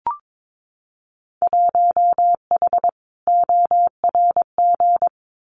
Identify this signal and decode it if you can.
Morse code, heard as a keyed tone, about 22 wpm, 700 Hz. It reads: 1H ORG
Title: Winter Field Day Exchanges 22wpm